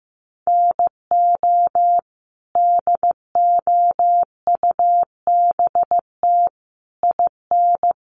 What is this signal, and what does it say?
Morse code, heard as a keyed tone, about 15 wpm, 700 Hz. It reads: NO DOUBT IN
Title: Sets of 5 Words Encore2 15wpm